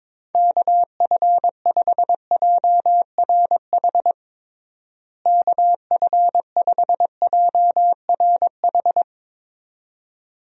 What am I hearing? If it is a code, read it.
Morse code, heard as a keyed tone, about 23 wpm, 700 Hz. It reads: KF5JRH KF5JRH
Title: USA Call Signs - 6 Characters - Repeated 2x - 22wpm